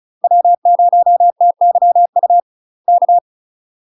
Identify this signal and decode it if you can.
Morse code, heard as a keyed tone, about 35 wpm, 700 Hz. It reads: W0TYU K
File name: UK MASTER QSOs TRAINING / 35WPM / UK MASTER QSOs 0007